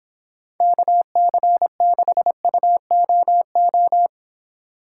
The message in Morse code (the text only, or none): KC6UOO